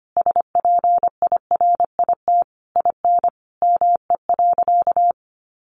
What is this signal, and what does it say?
Morse code, heard as a keyed tone, about 25 wpm, 700 Hz. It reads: SPIRIT IN ME.
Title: Sentences from Top 1400 Words 25wpm